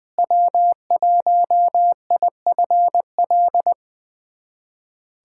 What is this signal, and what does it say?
Morse code, heard as a keyed tone, about 21 wpm, 700 Hz. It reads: W1IFL
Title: USA Callsign DE Callsign - 5 Characters - 20wpm